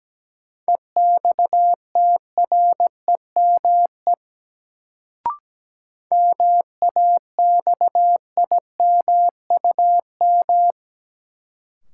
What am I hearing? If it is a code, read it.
Morse code, heard as a keyed tone, about 17 wpm, 700 Hz. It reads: EXTREME MAXIMUM
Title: Sentences from Top 1300 Words - New Word Drill - 17wpm